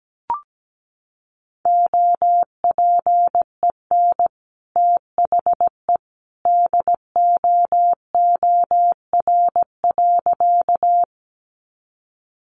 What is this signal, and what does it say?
Morse code, heard as a keyed tone, about 17 wpm, 700 Hz. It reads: OPEN THE DOOR.
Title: Sentences from Top 300 Words 17wpm